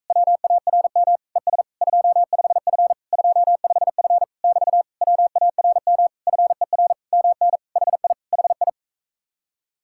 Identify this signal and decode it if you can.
Morse code, heard as a keyed tone, about 45 wpm, 700 Hz. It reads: WARM ES 25F 25F <BT> WARM FER MN HI HI